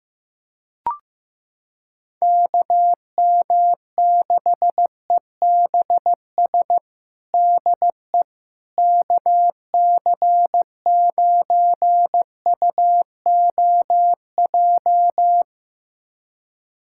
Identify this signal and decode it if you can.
Morse code, heard as a keyed tone, about 15 wpm, 700 Hz. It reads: KM6EBS DE KC9UOJ
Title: USA Callsign DE Callsign - 6 Characters - 15wpm